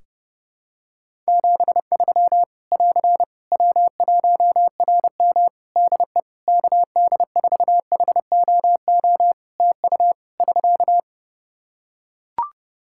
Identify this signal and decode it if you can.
Morse code, heard as a keyed tone, about 30 wpm, 700 Hz. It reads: 73 <AR> W1RM DE KD4HOO TU <SK>
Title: Practice QSO - Exchange 4 - Ending - 30wpm